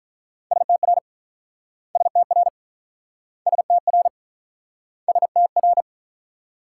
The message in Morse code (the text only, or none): STR STR STR STR